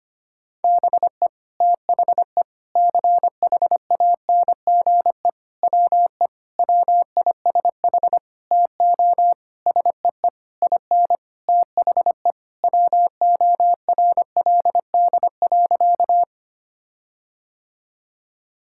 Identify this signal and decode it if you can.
Morse code, heard as a keyed tone, about 25 wpm, 700 Hz. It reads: BE THE CHANGE WE WISH TO SEE IN THE WORLD.
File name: practice / 25wpm / Sentences with 9-Word Phrases - 25wpm